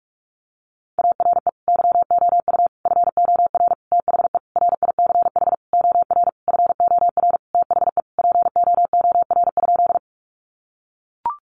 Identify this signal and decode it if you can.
Morse code, heard as a keyed tone, about 44 wpm, 700 Hz. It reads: ARE YOU FOR THE RICH OR FOR THE POOR?
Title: Sentences from Top 1100 Words 45wpm